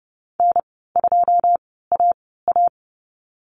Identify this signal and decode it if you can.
Morse code, heard as a keyed tone, about 30 wpm, spaced as for 17 wpm, 700 Hz. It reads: N2AA
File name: Lesson 14g - Number 2 - Call Signs 30wpm